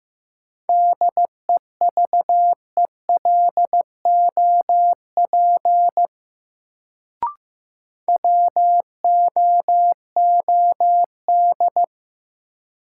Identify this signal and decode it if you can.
Morse code, heard as a keyed tone, about 15 wpm, 700 Hz. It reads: DEVELOP WOOD
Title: Sentences from Top 1000 Words - Review Part 2 - New Word Drill - 15wpm